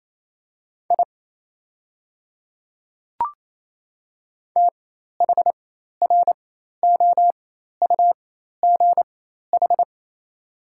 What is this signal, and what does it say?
Morse code, heard as a keyed tone, about 28 wpm, spaced as for 13 wpm, 700 Hz. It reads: I THROUGH
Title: 4x Character Spacing - Top 200 Words - 28wpm